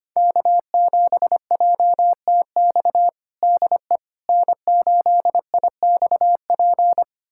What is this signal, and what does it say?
Morse code, heard as a keyed tone, about 25 wpm, 700 Hz. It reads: K7JTX DE N8IXP
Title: USA Callsign DE Callsign - 5 Characters - 25wpm